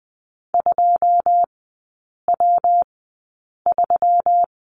Morse code, heard as a keyed tone, about 20 wpm, spaced as for 8 wpm, 690 Hz. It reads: 2W3
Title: 159 Lesson 26e - Letter g - Three Characters 20wpm